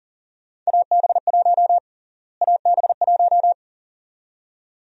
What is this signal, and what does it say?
Morse code, heard as a keyed tone, about 40 wpm, 700 Hz. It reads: AB1 AB1
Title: USA Call Sign Prefixes - Repeated 2x - 40wpm